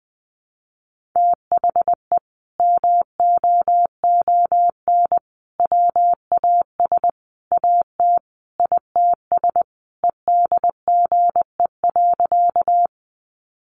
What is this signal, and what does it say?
Morse code, heard as a keyed tone, about 20 wpm, 700 Hz. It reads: THE MOON WAS AT ITS EDGE.
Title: Sentences from Top 1300 Words 20wpm